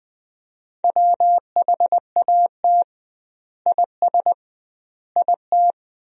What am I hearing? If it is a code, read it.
Morse code, heard as a keyed tone, about 20 wpm, 700 Hz. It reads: WHAT IS IT
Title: 2x Word-Spacing - Sentences from Top 100 Words - 20wpm